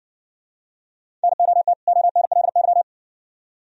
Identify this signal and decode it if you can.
Morse code, heard as a keyed tone, about 60 wpm, 700 Hz. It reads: NOT ONLY